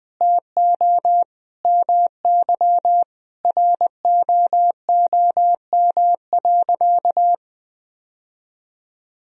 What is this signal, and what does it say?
Morse code, heard as a keyed tone, about 20 wpm, 700 Hz. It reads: TO MY ROOM.
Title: Sentences from Top 1000 Words - Review Part 3 - 20wpm